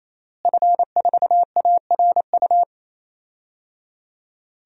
Morse code, heard as a keyed tone, about 27 wpm, 700 Hz. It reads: F4ARU
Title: DXCC-100 Call Signs - 5 Characters - Repeated 3x - 28wpm